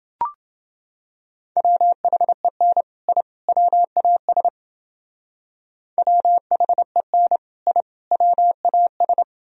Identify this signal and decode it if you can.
Morse code, heard as a keyed tone, about 29 wpm, 700 Hz. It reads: WHEN I WAS WHEN I WAS